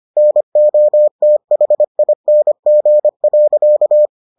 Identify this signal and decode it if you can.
Morse code, heard as a keyed tone, about 25 wpm, 600 Hz. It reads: NOTHING.